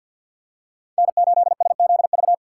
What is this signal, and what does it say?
Morse code, heard as a keyed tone, about 50 wpm, 700 Hz. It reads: N8SZV
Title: USA Call Signs - 5 Characters - 2x Character Spacing - 50wpm